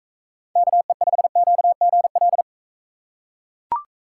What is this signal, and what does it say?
Morse code, heard as a keyed tone, about 38 wpm, 700 Hz. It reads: KE5QGL